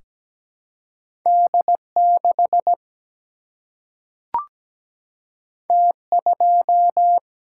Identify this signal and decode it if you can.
Morse code, heard as a keyed tone, about 17 wpm, 700 Hz. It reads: D 6 T 2
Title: Single-Letter Single-Number 17wpm